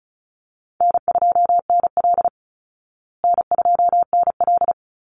Morse code, heard as a keyed tone, about 34 wpm, 700 Hz. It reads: N2NL N2NL
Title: Easy Call Signs Repeated 3x - 35wpm